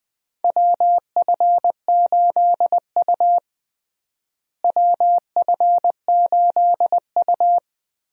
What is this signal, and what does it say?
Morse code, heard as a keyed tone, about 21 wpm, 700 Hz. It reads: WF8U WF8U
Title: USA Call Signs - 4 Characters - Repeated 2x - 20wpm